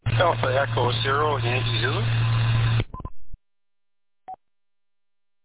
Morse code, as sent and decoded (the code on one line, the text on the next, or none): none